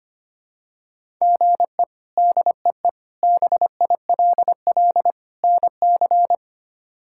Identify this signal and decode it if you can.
Morse code, heard as a keyed tone, about 25 wpm, 700 Hz. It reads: GE DEE BILL NC
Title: K1USN SST QSO Practice - 25wpm